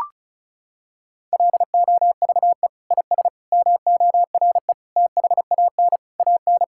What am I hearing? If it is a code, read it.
Morse code, heard as a keyed tone, about 36 wpm, 700 Hz. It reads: LOVE IS MORE THAN AN
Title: Sentences from Top 1600 Words 35wpm